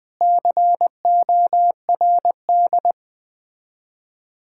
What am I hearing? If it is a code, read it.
Morse code, heard as a keyed tone, about 20 wpm, 690 Hz. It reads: CORD
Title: Difficult Words with Hints - 20wpm